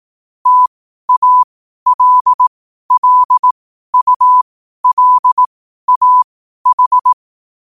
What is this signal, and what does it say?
Morse code, heard as a keyed tone, about 18 wpm, spaced as for 12 wpm, 1000 Hz. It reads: TALLULAH
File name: qotdfiles / 12wpm